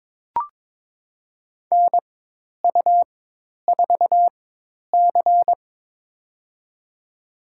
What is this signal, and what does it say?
Morse code, heard as a keyed tone, about 22 wpm, spaced as for 10 wpm, 700 Hz. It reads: NU4C
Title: USA Call Signs - 4x Character Spacing - 22wpm